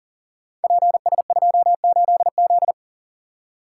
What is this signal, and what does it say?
Morse code, heard as a keyed tone, about 42 wpm, 690 Hz. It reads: PS28Z
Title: Code Groups - 5 Letter:Numbers - 40wpm